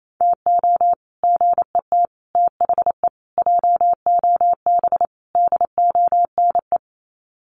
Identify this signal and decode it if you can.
Morse code, heard as a keyed tone, about 28 wpm, 700 Hz. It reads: TO GET THE JOB DONE
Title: Sets of 5 Words Encore2 28wpm